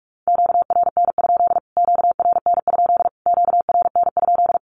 Morse code, heard as a keyed tone, about 46 wpm, 700 Hz. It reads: QRN? QRN? QRN?
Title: Q-Codes Repeated 3x 45wpm